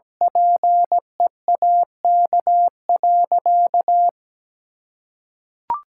Morse code, heard as a keyed tone, about 17 wpm, 700 Hz. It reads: PEAK.
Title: Sentences from Top 1600 Words 17wpm